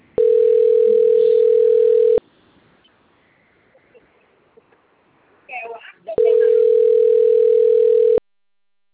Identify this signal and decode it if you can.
Ringback tone (North America), 440 + 480 Hz, with speech.
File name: rg-1600-unknown- - - .93686